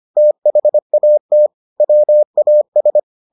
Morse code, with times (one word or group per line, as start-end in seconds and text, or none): 0.2-1.5 THAT
1.8-3.0 WAS